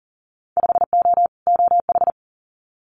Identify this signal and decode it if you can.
Morse code, heard as a keyed tone, about 41 wpm, 700 Hz. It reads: 5O OH
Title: Winter Field Day Exchanges Encore 40wpm